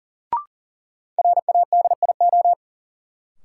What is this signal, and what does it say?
Morse code, heard as a keyed tone, about 40 wpm, 700 Hz. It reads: RADIO
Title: Sentences from Top 1000 Words - Review Part 3 - New Word Drill - 40wpm